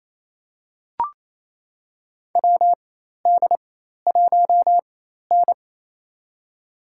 Morse code, heard as a keyed tone, about 28 wpm, spaced as for 13 wpm, 700 Hz. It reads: WD1N
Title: USA Call Signs - 4 Characters - 4x Character Spacing - 28wpm